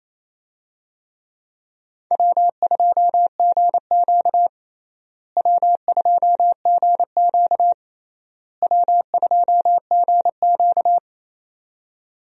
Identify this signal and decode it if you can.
Morse code, heard as a keyed tone, about 28 wpm, 700 Hz. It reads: W2GQ W2GQ W2GQ
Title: USA Call Signs - 4 Characters - Repeated 3x - 28wpm